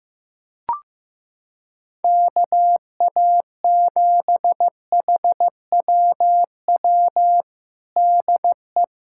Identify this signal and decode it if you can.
Morse code, heard as a keyed tone, about 15 wpm, 700 Hz. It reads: KA7HWW DE
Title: USA Callsign DE Callsign 15wpm